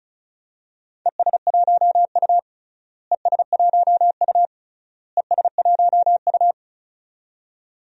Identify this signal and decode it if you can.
Morse code, heard as a keyed tone, about 34 wpm, 700 Hz. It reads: ES1U ES1U ES1U
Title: DXCC-100 Call Signs - 4 Characters - Repeated 3x - 35wpm